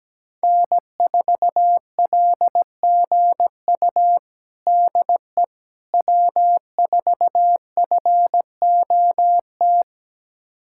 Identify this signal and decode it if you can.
Morse code, heard as a keyed tone, about 17 wpm, 700 Hz. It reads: N4LGU DE W4FOT
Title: USA Callsign DE Callsign - 5 Characters - 17wpm